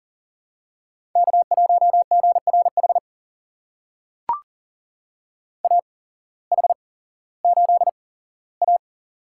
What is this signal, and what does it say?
Morse code, heard as a keyed tone, about 40 wpm, spaced as for 19 wpm, 700 Hz. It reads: K1GRH AH8A
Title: USA Call Signs - 8x Character Spacing - 40wpm